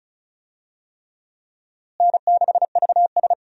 Morse code, heard as a keyed tone, about 36 wpm, 700 Hz. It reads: N6VS
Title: USA Call Signs - 4 Characters - Repeated 3x - 35wpm